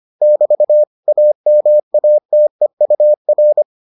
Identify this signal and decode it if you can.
Morse code, heard as a keyed tone, about 25 wpm, 600 Hz. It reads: <BT> AMATEUR